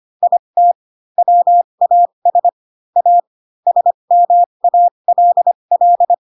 Morse code, heard as a keyed tone, about 25 wpm, spaced as for 22 wpm, 700 Hz. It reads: IT WAS A SMALL
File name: Pinocchio / Pinocchio ch3 18 25 ik1hge